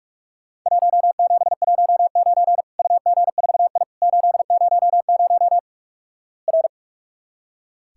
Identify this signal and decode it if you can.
Morse code, heard as a keyed tone, about 46 wpm, 700 Hz. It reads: 1719 UG4I 800 R
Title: Worked All Europe DX Contest - QTC Exchange - 45wpm